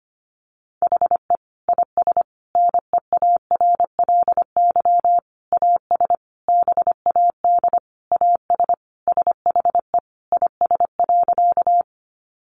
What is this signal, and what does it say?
Morse code, heard as a keyed tone, about 25 wpm, 700 Hz. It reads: HE IS NEARLY AS BAD AS SHE IS.
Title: Sentences from Top 1300 Words 25wpm